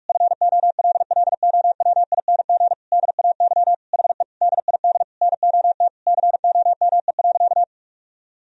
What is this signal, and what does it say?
Morse code, heard as a keyed tone, about 46 wpm, 700 Hz. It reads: FOLLOWING DAY HE DID NOT COME.
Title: Sentences from Top 1000 Words - Review Part 4 - 45wpm